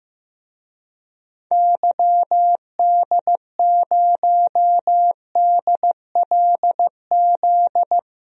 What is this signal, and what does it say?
Morse code, heard as a keyed tone, about 15 wpm, 700 Hz. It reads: YD0DLZ
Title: DXCC-100 Call Signs - 6 Characters - Repeated 3x - 15wpm